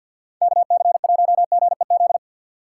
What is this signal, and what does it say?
Morse code, heard as a keyed tone, about 51 wpm, 700 Hz. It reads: KC1GEZ
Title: USA Call Signs - 6 Characters - 6x Character Spacing - 50wpm